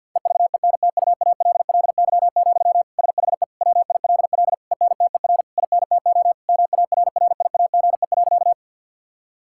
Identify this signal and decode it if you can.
Morse code, heard as a keyed tone, about 50 wpm, 700 Hz. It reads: EVENTUALLY, SHE WILL ENTER INTO MARRIAGE.
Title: Sentences from Top 1000 Words - Review Part 6 - 50wpm